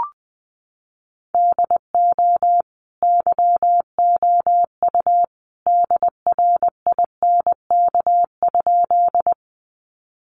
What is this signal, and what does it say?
Morse code, heard as a keyed tone, about 20 wpm, 700 Hz. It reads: DO YOU DRINK?
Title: Sentences from Top 1500 Words 20wpm